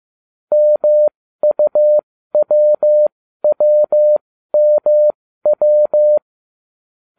Morse code, heard as a keyed tone, about 15 wpm, spaced as for 13 wpm, 600 Hz. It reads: MUWWMW